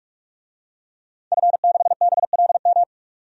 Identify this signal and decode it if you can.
Morse code, heard as a keyed tone, about 46 wpm, 700 Hz. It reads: F6BLM
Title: DXCC-100 Call Signs - 5 Characters - 45wpm